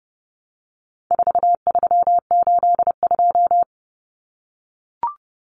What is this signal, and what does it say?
Morse code, heard as a keyed tone, about 30 wpm, 700 Hz. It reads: 4382
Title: Common Cut Numbers - 4-Digit 30wpm